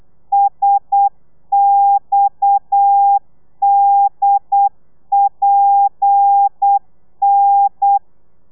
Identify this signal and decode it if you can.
Morse code, heard as a keyed tone, about 8 wpm, 800 Hz. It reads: SXDPN